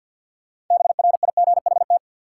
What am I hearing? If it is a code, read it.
Morse code, heard as a keyed tone, about 48 wpm, 700 Hz. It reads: BRIGHT
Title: Sentences from Top 1000 Words - New Word Drill - 50wpm